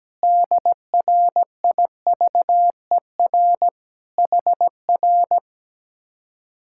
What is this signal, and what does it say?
Morse code, heard as a keyed tone, about 17 wpm, 700 Hz. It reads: DRIVER HR
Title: Speed-Racing Common QSO Phrases - 17wpm